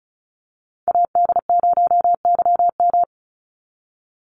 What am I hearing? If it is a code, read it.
Morse code, heard as a keyed tone, about 34 wpm, 700 Hz. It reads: AD0YM
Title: NA SOTA Chaser Callsigns 35wpm